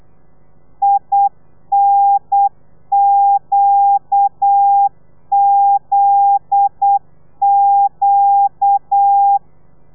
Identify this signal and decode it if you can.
Morse code, heard as a keyed tone, about 8 wpm, 800 Hz. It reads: INQZQ
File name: telegrafia / i2tfj / Lez9 35